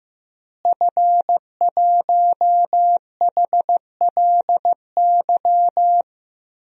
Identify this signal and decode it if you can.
Morse code, heard as a keyed tone, about 15 wpm, 700 Hz. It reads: F1HLY
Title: DXCC-100 Call Signs - 5 Characters - Repeated 3x - 15wpm